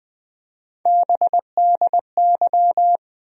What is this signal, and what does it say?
Morse code, heard as a keyed tone, about 20 wpm, 700 Hz. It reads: BDY
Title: USA Call Sign Postfixes - 3 Letters 20wpm